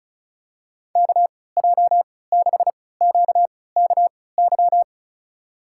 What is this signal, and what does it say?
Morse code, heard as a keyed tone, about 36 wpm, spaced as for 20 wpm, 700 Hz. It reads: KJ6QKY
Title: USA Call Signs - 6 Characters - 3x Character Spacing - 35wpm